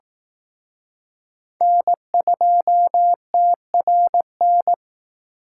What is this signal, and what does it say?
Morse code, heard as a keyed tone, about 19 wpm, 700 Hz. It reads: N2TRN
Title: Speed Racing - USA 1x3 Call Signs 15wpm